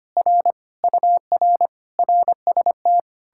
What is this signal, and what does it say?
Morse code, heard as a keyed tone, about 33 wpm, 700 Hz. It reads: R UR RST